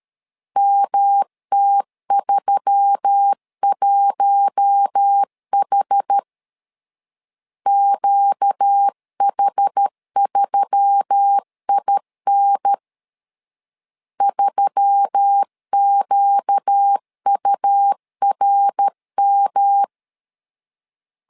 Morse code, heard as a keyed tone, about 13 wpm, 800 Hz. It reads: MT31H QH3IN 3QURM
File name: LEZIONI / 4 Q1S3 12wpm 2023